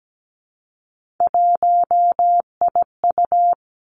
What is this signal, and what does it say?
Morse code, heard as a keyed tone, about 17 wpm, 700 Hz. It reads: 1IU
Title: Three Challenging Letter-Number Combinations 17wpm